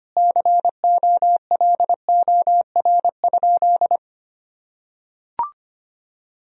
Morse code, heard as a keyed tone, about 25 wpm, 700 Hz. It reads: COLOR?
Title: Sentences from Top 1000 Words - Review Part 2 - 25wpm